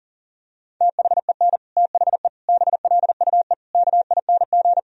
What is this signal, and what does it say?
Morse code, heard as a keyed tone, about 40 wpm, 700 Hz. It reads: THEN THE BLUE KING
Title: Sentences from Top 1000 Words - Review Part 3 - 40wpm